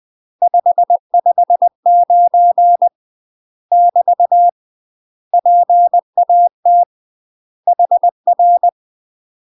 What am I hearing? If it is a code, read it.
Morse code, heard as a keyed tone, about 20 wpm, 700 Hz. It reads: 559 <BT> PAT HR
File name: UK MASTER QSOs TRAINING / 20WPM / UK MASTER QSOs 0025